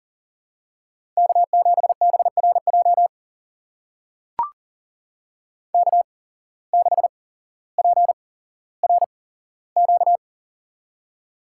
Morse code, heard as a keyed tone, about 40 wpm, spaced as for 17 wpm, 700 Hz. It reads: K7BRJ K6PRQ
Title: USA Call Signs - 5 Characters - 8x Character Spacing - 40wpm